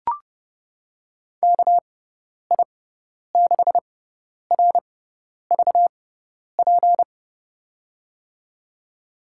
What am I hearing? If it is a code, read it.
Morse code, heard as a keyed tone, about 32 wpm, spaced as for 10 wpm, 700 Hz. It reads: KI6RVP
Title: USA Call Signs - 6 Characters - 6x Character Spacing - 30wpm